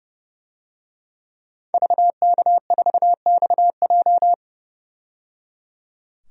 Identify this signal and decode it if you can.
Morse code, heard as a keyed tone, about 30 wpm, 700 Hz. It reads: VK4XJ
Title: DXCC-100 Call Signs - 5 Characters - 30wpm